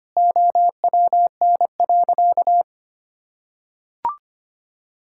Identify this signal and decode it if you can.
Morse code, heard as a keyed tone, about 25 wpm, 700 Hz. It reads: OWN.
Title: Sentences from Top 1000 Words - Review Part 6 - 25wpm